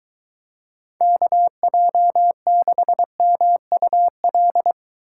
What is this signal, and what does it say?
Morse code, heard as a keyed tone, about 23 wpm, 700 Hz. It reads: KJ6MUL